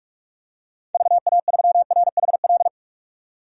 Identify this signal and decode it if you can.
Morse code, heard as a keyed tone, about 46 wpm, 690 Hz. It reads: VA3RHL
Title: DXCC-100 Call Signs - 6 Characters - 45wpm